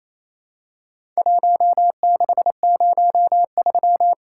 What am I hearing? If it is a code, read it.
Morse code, heard as a keyed tone, about 28 wpm, 700 Hz. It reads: 1603
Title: Worked All Europe DX Contest - QTC Exchange - 28wpm